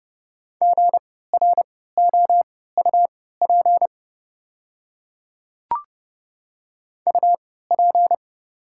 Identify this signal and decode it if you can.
Morse code, heard as a keyed tone, about 30 wpm, spaced as for 17 wpm, 690 Hz. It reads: GROUP UP
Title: 3x Character Spacing - Top 500 Words - 30wpm